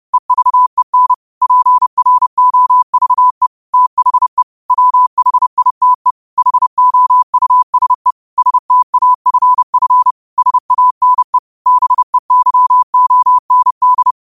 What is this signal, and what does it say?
Morse code, heard as a keyed tone, about 30 wpm, 1000 Hz. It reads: EVEN PROVE THE WHITE HOUSE STAFF SANE BEYOND